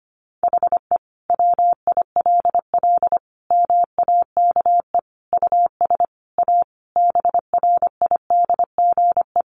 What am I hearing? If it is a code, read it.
Morse code, heard as a keyed tone, about 25 wpm, 700 Hz. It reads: HE WILL MAKE US A BRIDGE
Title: Sentences from Top 1200 Words 25wpm